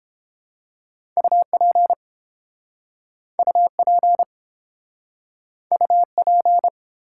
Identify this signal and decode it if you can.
Morse code, heard as a keyed tone, about 30 wpm, 700 Hz. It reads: UP UP UP